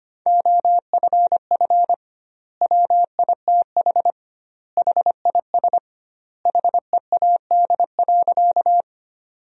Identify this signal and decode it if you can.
Morse code, heard as a keyed tone, about 25 wpm, 700 Hz. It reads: OFF WITH HIS HEAD.
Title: 2x Word-Spacing - Sentences from Top 300 Words - 25wpm